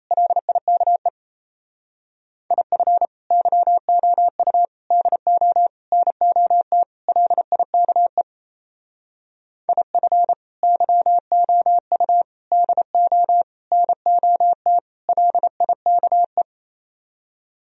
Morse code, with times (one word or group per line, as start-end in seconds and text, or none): 0.1-1.1 LIKE
2.5-3.1 IF
3.3-4.7 YOU
4.9-5.7 DO
5.9-6.8 NOT
7.1-8.2 LIKE
9.7-10.3 IF
10.6-12.2 YOU
12.5-13.4 DO
13.7-14.8 NOT
15.1-16.4 LIKE